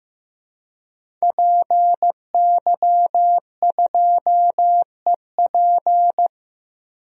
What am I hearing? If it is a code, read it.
Morse code, heard as a keyed tone, about 16 wpm, 700 Hz. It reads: PY2EP
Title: DXCC-100 Call Signs - 5 Characters - 15wpm